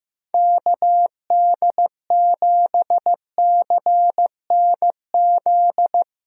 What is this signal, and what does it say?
Morse code, heard as a keyed tone, about 15 wpm, 700 Hz. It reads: KD7CNZ
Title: USA Call Signs - 6 Characters - Repeated 2x - 15wpm